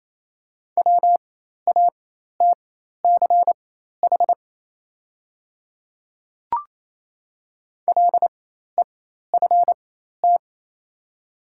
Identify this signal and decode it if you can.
Morse code, heard as a keyed tone, about 28 wpm, spaced as for 13 wpm, 700 Hz. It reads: WATCH LEFT
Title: 4x Character Spacing - Top 500 Words - 28wpm